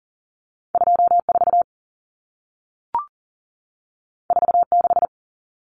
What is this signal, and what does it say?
Morse code, heard as a keyed tone, about 40 wpm, 700 Hz. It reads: 24 46
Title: Numbers - 2-Digit 40wpm